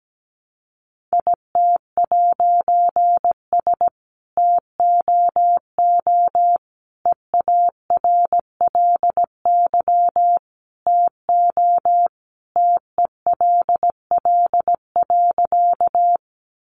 Morse code, heard as a keyed tone, about 17 wpm, 700 Hz. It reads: IT'S TOO EARLY TO TELL.
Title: A Sound Of Thunder 17wpm